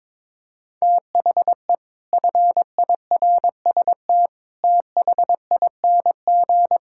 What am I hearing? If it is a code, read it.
Morse code, heard as a keyed tone, about 22 wpm, 700 Hz. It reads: THE FIRST THING
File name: practice / 22wpm / Sentences with 8-Word Phrases - 22wpm